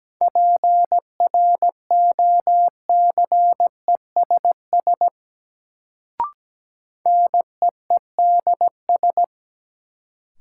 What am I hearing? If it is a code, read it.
Morse code, heard as a keyed tone, about 17 wpm, 700 Hz. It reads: PROCESS NEEDS
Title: Sentences from Top 1000 Words - Review Part 3 - New Word Drill - 17wpm